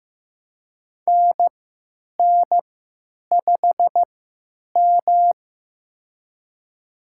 Morse code, heard as a keyed tone, about 15 wpm, spaced as for 9 wpm, 700 Hz. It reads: NN5M
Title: USA Call Signs - 4 Characters - 3x Character Spacing - 15wpm